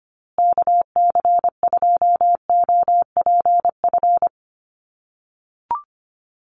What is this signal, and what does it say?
Morse code, heard as a keyed tone, about 25 wpm, 700 Hz. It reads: KC2OPF
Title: USA Call Signs Repeated 2x 25wpm